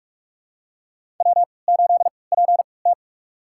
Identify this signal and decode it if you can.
Morse code, heard as a keyed tone, about 41 wpm, spaced as for 25 wpm, 700 Hz. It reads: W8PT
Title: USA Call Signs - 4 Characters - 3x Character Spacing - 45wpm